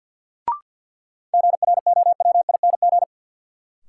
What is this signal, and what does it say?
Morse code, heard as a keyed tone, about 53 wpm, 700 Hz. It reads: GROWING
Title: Sets of 5 Words Encore - New Word Drill - 50wpm